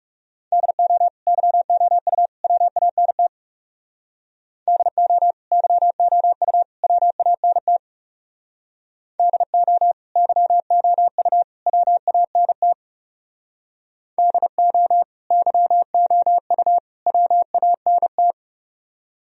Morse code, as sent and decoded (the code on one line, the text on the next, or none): -.. --- / -.-- --- ..- / .-- .- -. - / -.. --- / -.-- --- ..- / .-- .- -. - / -.. --- / -.-- --- ..- / .-- .- -. - / -.. --- / -.-- --- ..- / .-- .- -. -
DO YOU WANT DO YOU WANT DO YOU WANT DO YOU WANT